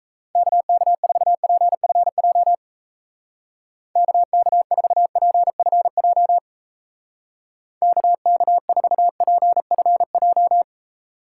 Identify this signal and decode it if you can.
Morse code, heard as a keyed tone, about 33 wpm, 700 Hz. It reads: KK4PFJ KK4PFJ KK4PFJ
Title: Speed Racing - USA 2x3 Call Signs 28wpm